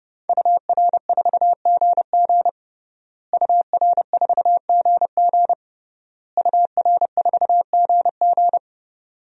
Morse code, heard as a keyed tone, about 30 wpm, 700 Hz. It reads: UR4GG UR4GG UR4GG
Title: DXCC-100 Call Signs - 5 Characters - Repeated 3x - 30wpm